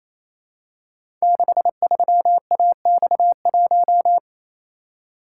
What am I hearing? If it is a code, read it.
Morse code, heard as a keyed tone, about 28 wpm, 700 Hz. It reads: 63AX1
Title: Code Groups - 5 Letter:Numbers - 28wpm